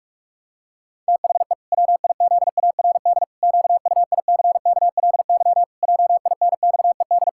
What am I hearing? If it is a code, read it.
Morse code, heard as a keyed tone, about 41 wpm, 700 Hz. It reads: THE WIZARD QUICKLY JINXED
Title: Pangram Challenge v2 - 45wpm